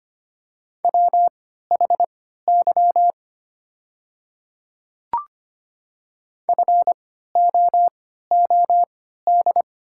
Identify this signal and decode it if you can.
Morse code, heard as a keyed tone, about 25 wpm, spaced as for 14 wpm, 700 Hz. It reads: WHY FOOD
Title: 3x Character Spacing - Top 300 Words - 25wpm